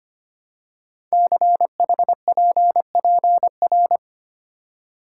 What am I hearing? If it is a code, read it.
Morse code, heard as a keyed tone, about 25 wpm, 700 Hz. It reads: CHPPR
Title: Code Groups - 5 Letters - 25wpm